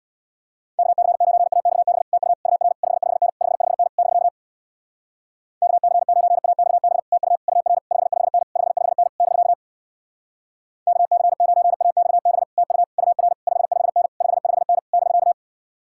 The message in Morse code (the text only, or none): KC9AXB TU UR 55N 55N <BK> KC9AXB TU UR 55N 55N <BK> KC9AXB TU UR 55N 55N <BK>